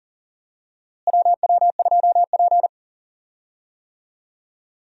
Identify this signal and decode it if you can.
Morse code, heard as a keyed tone, about 40 wpm, 700 Hz. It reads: WW2P
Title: USA Call Signs - 4 Characters - Repeated 2x - 40wpm